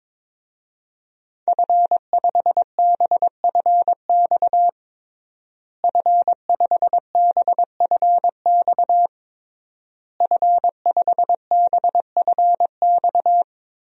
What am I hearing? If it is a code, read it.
Morse code, heard as a keyed tone, about 22 wpm, 700 Hz. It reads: F5BFX F5BFX F5BFX
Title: DXCC-100 Call Signs - 5 Characters - Repeated 3x - 22wpm